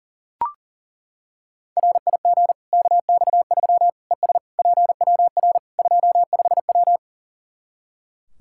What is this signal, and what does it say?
Morse code, heard as a keyed tone, about 40 wpm, 700 Hz. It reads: RIG KX3 ES PWR 25W
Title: K1USN SST QSO Practice 40wpm